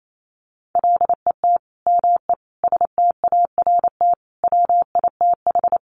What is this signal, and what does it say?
Morse code, heard as a keyed tone, about 28 wpm, 700 Hz. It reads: LET ME START WITH
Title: Sets of 5 Words Encore 28wpm